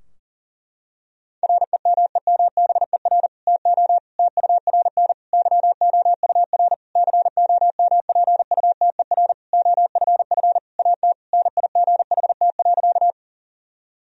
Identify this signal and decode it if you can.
Morse code, heard as a keyed tone, about 40 wpm, 700 Hz. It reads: REMEMBER TO TURN YOUR COMPUTER OFF AT NIGHT.
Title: Sentences from Top 1000 Words - Review Part 3 - 40wpm